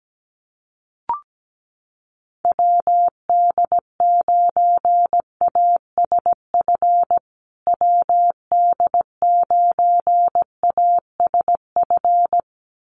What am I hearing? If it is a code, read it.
Morse code, heard as a keyed tone, about 17 wpm, 700 Hz. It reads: WD9ASF WD9ASF
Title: USA Call Signs Repeated 2x 17wpm